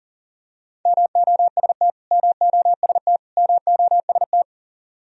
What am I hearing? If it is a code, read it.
Morse code, heard as a keyed tone, about 40 wpm, 700 Hz. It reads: MOST MOST MOST
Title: Top 100 Words Repeated 40wpm